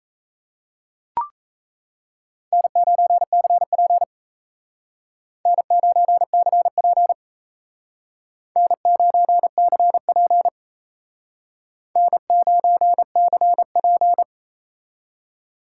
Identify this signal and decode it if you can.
Morse code, heard as a keyed tone, about 36 wpm, 700 Hz. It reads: N9CP N9CP N9CP N9CP